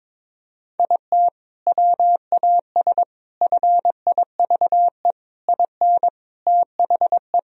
Morse code, heard as a keyed tone, about 21 wpm, 700 Hz. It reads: IT WAS FIVE IN THE
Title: Sentences from Top 1000 Words - Review Part 2 - 22wpm